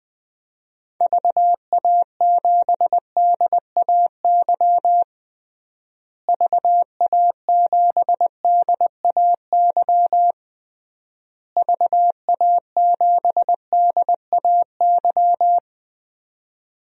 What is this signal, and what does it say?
Morse code, heard as a keyed tone, about 21 wpm, 700 Hz. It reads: VA7DAY VA7DAY VA7DAY
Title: DXCC-100 Call Signs - 6 Characters - Repeated 3x - 20wpm